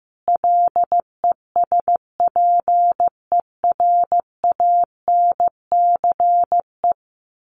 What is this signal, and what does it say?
Morse code, heard as a keyed tone, about 15 wpm, 700 Hz. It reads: LESPERANCE
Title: A Sound Of Thunder 15wpm